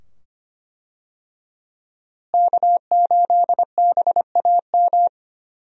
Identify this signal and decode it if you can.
Morse code, heard as a keyed tone, about 25 wpm, 700 Hz. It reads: K8BAM
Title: USA Call Signs - 5 Characters - Repeated 2x - 25wpm